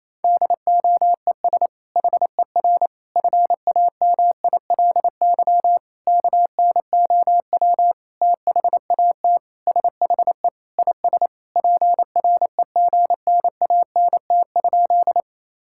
Morse code, heard as a keyed tone, about 27 wpm, 700 Hz. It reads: DOES HER FAMILY KNOW THAT SHE IS PREGNANT?
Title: Sentences from Top 1000 Words - Review Part 6 - 28wpm